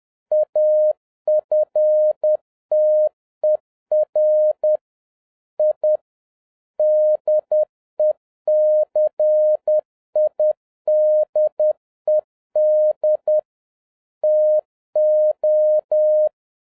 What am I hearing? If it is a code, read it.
Morse code, heard as a keyed tone, about 10 wpm, 610 Hz. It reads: AFTER I DECIDED TO